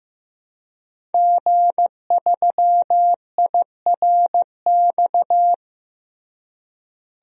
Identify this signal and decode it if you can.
Morse code, heard as a keyed tone, about 16 wpm, 700 Hz. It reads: G3IRX
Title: DXCC-100 Call Signs - 5 Characters - Repeated 3x - 15wpm